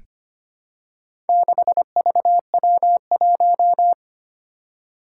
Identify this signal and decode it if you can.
Morse code, heard as a keyed tone, about 25 wpm, 700 Hz. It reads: 6VW1
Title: Four Challenging Letter-Number Combinations 25wpm